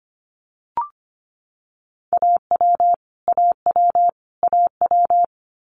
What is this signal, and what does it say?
Morse code, heard as a keyed tone, about 25 wpm, 700 Hz. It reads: AW AW AW
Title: Multi-letter Phonemes Repeated 3x 25wpm